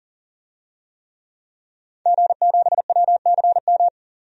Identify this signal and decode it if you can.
Morse code, heard as a keyed tone, about 40 wpm, 700 Hz. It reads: G7WCM
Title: DXCC-100 Call Signs - 5 Characters - 40wpm